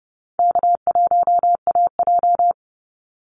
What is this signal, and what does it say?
Morse code, heard as a keyed tone, about 30 wpm, 700 Hz. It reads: K1AJ
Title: Easy Call Signs Repeated 2x - 30wpm